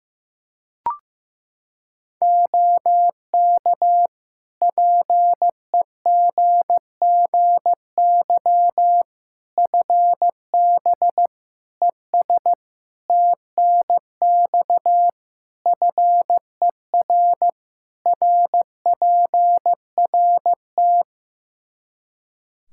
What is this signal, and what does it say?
Morse code, heard as a keyed tone, about 15 wpm, 700 Hz. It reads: OK PEGGY FB ES TNX FER RPRT
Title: Practice QSO - Exchange 2 - Info - 15wpm